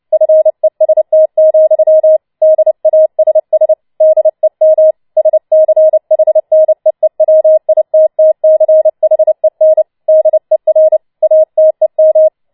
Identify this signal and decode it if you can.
Morse code, heard as a keyed tone, about 29 wpm, 630 Hz. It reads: FEST, DASS DEM SCHNEEWITTCHEN DER ATEM
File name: Schneewittchen Teil3 30wpm